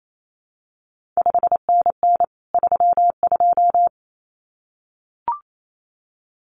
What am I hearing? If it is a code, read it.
Morse code, heard as a keyed tone, about 28 wpm, 700 Hz. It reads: 5NN 32
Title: CQ WW DX Contest Exchange - 28wpm